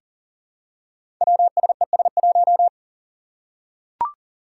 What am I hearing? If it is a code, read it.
Morse code, heard as a keyed tone, about 42 wpm, 700 Hz. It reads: WSES1
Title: Five Challenging Letter-Number Combinations 40wpm